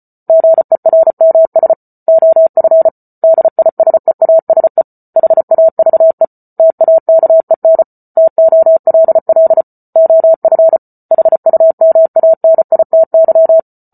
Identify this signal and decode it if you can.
Morse code, heard as a keyed tone, about 34 wpm, 650 Hz. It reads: GERMS OF DISEASE HAVE TAKEN TOLL OF HUMANITY
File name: Book Two / 08-Chapter 8 - 34 WPM